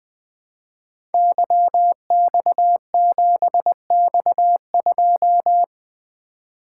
Morse code, heard as a keyed tone, about 20 wpm, 700 Hz. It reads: YX7X2
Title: Code Groups - 5 Letter:Numbers - 20wpm